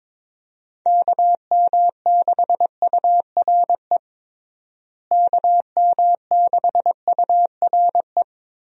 Morse code, heard as a keyed tone, about 21 wpm, 700 Hz. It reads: KM6URE KM6URE
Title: USA Call Signs - 6 Characters - Repeated 3x - 22wpm